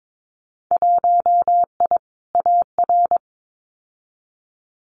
Morse code, heard as a keyed tone, about 22 wpm, 700 Hz. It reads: 1I AR
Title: Winter Field Day Exchanges Encore 22wpm